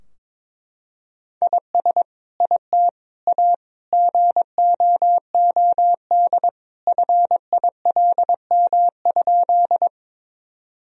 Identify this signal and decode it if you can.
Morse code, heard as a keyed tone, about 22 wpm, 700 Hz. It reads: IS IT A GOOD FILM?
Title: Sentences from Top 1000 Words - Review Part 3 - 22wpm